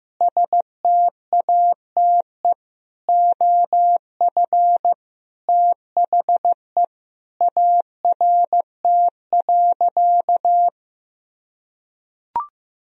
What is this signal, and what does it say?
Morse code, heard as a keyed tone, about 15 wpm, 700 Hz. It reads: STATE OF THE ART.
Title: Sentences from Top 1000 Words - Review Part 3 - 15wpm